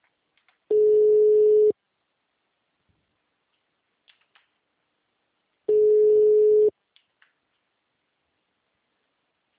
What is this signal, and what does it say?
Ringback tone (Europe, 425 Hz).
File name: out- -2011- - - .433